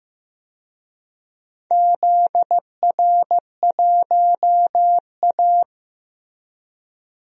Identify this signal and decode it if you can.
Morse code, heard as a keyed tone, about 15 wpm, 700 Hz. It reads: ZR1A
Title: DXCC-100 Call Signs - 4 Characters - 15wpm